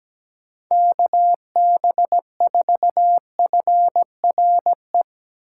Morse code, heard as a keyed tone, about 17 wpm, 700 Hz. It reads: KB4FRE
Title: USA Call Signs - 6 Characters - 2x Character Spacing - 17wpm